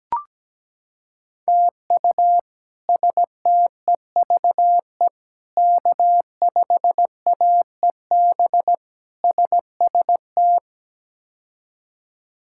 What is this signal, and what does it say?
Morse code, heard as a keyed tone, about 17 wpm, 700 Hz. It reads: TU STEVE K5AEB SST
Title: K1USN SST QSO Practice - 17wpm